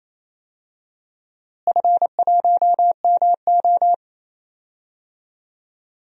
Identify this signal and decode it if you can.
Morse code, heard as a keyed tone, about 28 wpm, 700 Hz. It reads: F1MO
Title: DXCC-100 Call Signs - 4 Characters - Repeated 3x - 28wpm